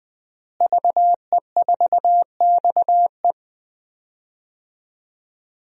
Morse code, heard as a keyed tone, about 20 wpm, 700 Hz. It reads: VE4XE
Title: DXCC-100 Call Signs - 5 Characters - Repeated 2x - 20wpm